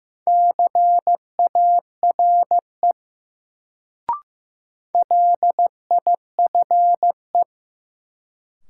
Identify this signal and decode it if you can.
Morse code, heard as a keyed tone, about 15 wpm, 700 Hz. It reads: CARE LIFE
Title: Sentences from Top 300 Words - New Word Drill - 15wpm